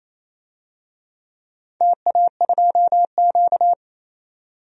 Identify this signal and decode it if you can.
Morse code, heard as a keyed tone, about 28 wpm, 700 Hz. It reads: TA2Q
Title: DXCC-100 Call Signs - 4 Characters - 28wpm